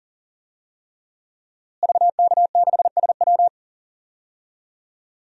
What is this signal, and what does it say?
Morse code, heard as a keyed tone, about 40 wpm, 700 Hz. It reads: VK6SW
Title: DXCC-100 Call Signs - 5 Characters - Repeated 3x - 40wpm